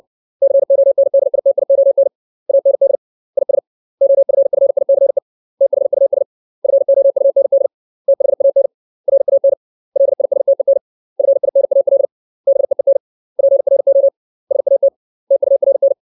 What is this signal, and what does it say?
Morse code, heard as a keyed tone, about 60 wpm, 550 Hz. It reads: CONDITION AND IS OFFICE THUS FOUND THAN RAN LISTEN FINAL BEEN PRO SAT TURN
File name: 632 words / 632 Words 60WPM